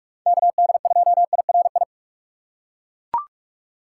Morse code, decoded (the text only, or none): KD2IRI